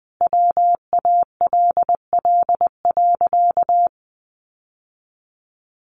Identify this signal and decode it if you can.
Morse code, heard as a keyed tone, about 20 wpm, 700 Hz. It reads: WALL.
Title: Sentences from Top 1100 Words 20wpm